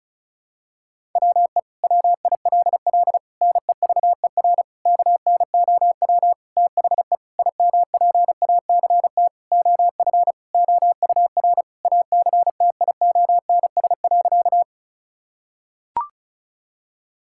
Morse code, heard as a keyed tone, about 34 wpm, 700 Hz. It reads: WE WILL NEVER KNOW THE IMPACT OF OUR ACTIONS.
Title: Sentences from Top 1000 Words - Review Part 6 - 35wpm